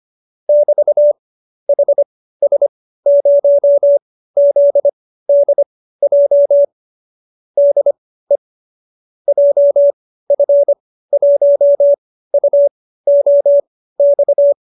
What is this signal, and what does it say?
Morse code, heard as a keyed tone, about 25 wpm, spaced as for 15 wpm, 590 Hz. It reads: <BT> HS0ZDJ DE JF1UOX